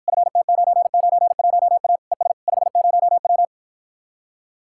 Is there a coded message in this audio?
Yes: Morse code from 0.1 to 3.5 s, heard as a keyed tone, about 54 wpm, 700 Hz. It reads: FT991A ES 50W